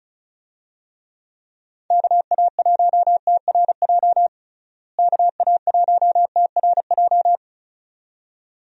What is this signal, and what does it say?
Morse code, heard as a keyed tone, about 34 wpm, 700 Hz. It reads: KA1TRJ KA1TRJ
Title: USA Call Signs - 6 Characters - Repeated 2x - 35wpm